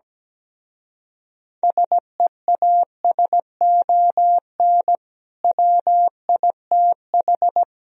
Morse code, heard as a keyed tone, about 17 wpm, 700 Hz. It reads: SEASON WITH
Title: Speed-Racing - Sets of 5 Words 17wpm